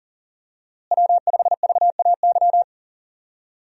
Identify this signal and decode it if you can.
Morse code, heard as a keyed tone, about 40 wpm, 700 Hz. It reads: W5VAY
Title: USA Call Signs - 5 Characters - Repeated 2x - 40wpm